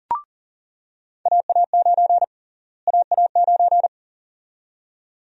Morse code, heard as a keyed tone, about 40 wpm, 700 Hz. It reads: AA9 AA9
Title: USA Call Sign Prefixes - Repeated 2x - 40wpm